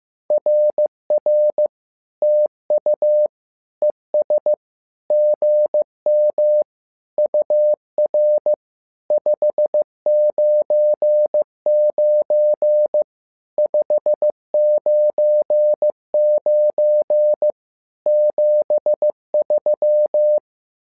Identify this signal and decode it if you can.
Morse code, heard as a keyed tone, about 15 wpm, 600 Hz. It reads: RR TU ES GM UR 599 599 73
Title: SOTA QSO Practice 15wpm